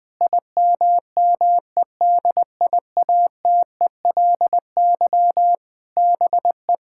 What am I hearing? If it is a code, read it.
Morse code, heard as a keyed tone, about 21 wpm, 700 Hz. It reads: IMMEDIATELY BE
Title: Sentences with 7-Word Phrases - Spoken Beginning - 20wpm